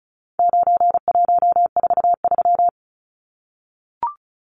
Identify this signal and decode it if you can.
Morse code, heard as a keyed tone, about 34 wpm, 700 Hz. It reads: 9143
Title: Numbers - 4-Digit 35wpm